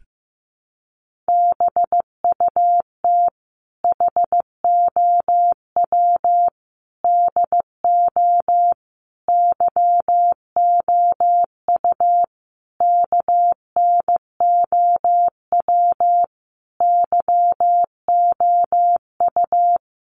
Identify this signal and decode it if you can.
Morse code, heard as a keyed tone, about 15 wpm, 700 Hz. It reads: BUT HOW DO YOU KNOW YOU
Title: Sentences from Top 900 Words 15wpm